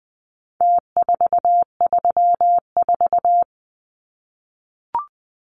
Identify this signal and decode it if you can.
Morse code, heard as a keyed tone, about 20 wpm, 700 Hz. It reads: T434
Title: Numbers - Contest Style - 4-Digit 20wpm